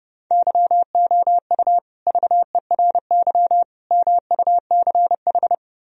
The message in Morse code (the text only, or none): YOU VERY MUCH